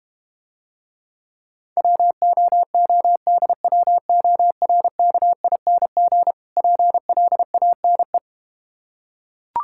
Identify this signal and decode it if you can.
Morse code, heard as a keyed tone, about 32 wpm, 700 Hz. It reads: WOODWORKING PLANE